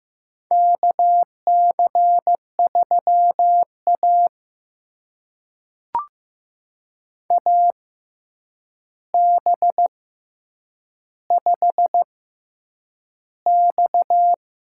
Morse code, heard as a keyed tone, about 15 wpm, spaced as for 8 wpm, 700 Hz. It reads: KC3A AB5X
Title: USA Call Signs - 4 Characters - 6x Character Spacing - 15wpm